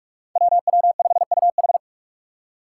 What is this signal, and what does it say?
Morse code, heard as a keyed tone, about 43 wpm, 700 Hz. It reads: WW5UH
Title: Five Challenging Letter-Number Combinations 45wpm